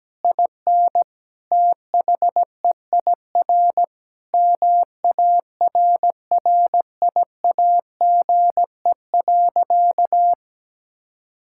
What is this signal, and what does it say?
Morse code, heard as a keyed tone, about 17 wpm, 700 Hz. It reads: IN THEIR MARRIAGE.
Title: Sentences from Top 1000 Words - Review Part 6 - 17wpm